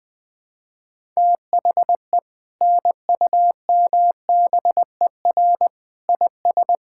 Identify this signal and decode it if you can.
Morse code, heard as a keyed tone, about 20 wpm, 700 Hz. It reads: THE NUMBER IS
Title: Sentences with 2-Word Phrases - 20wpm